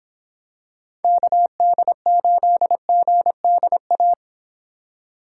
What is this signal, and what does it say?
Morse code, heard as a keyed tone, about 26 wpm, 700 Hz. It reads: KD8GDA